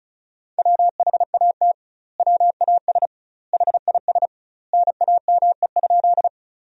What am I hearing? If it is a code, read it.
Morse code, heard as a keyed tone, about 36 wpm, 700 Hz. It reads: WHAT WAS HIS NAME?
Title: 2x Word-Spacing - Sentences from Top 200 Words - 35wpm